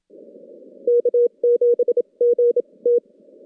Morse code, heard as a keyed tone, about 27 wpm, spaced as for 22 wpm, 480 Hz. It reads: K7GT
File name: contests / CW-SS-k5tr- -2100